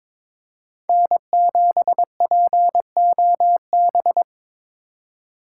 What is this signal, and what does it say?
Morse code, heard as a keyed tone, about 22 wpm, 700 Hz. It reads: N7POB
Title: USA Call Signs - 4x Character Spacing - 22wpm